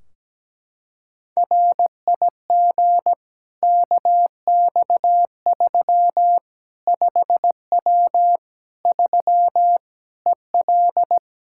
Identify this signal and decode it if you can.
Morse code, heard as a keyed tone, about 17 wpm, 700 Hz. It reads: RIG KX3 5W 3 EL
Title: K1USN SST QSO Practice 17wpm